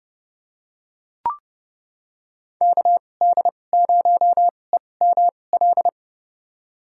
Morse code, heard as a keyed tone, about 29 wpm, spaced as for 22 wpm, 700 Hz. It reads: KD0EML
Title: USA Call Signs - 2x Character Spacing - 30wpm